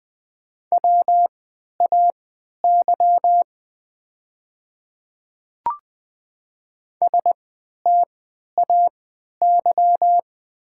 Morse code, heard as a keyed tone, about 20 wpm, spaced as for 11 wpm, 700 Hz. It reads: WAY STAY
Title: 3x Character Spacing - Top 500 Words - 20wpm